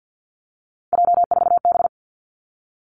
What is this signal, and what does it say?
Morse code, heard as a keyed tone, about 50 wpm, 700 Hz. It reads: J4B
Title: Three Challenging Letter-Number Combinations 50wpm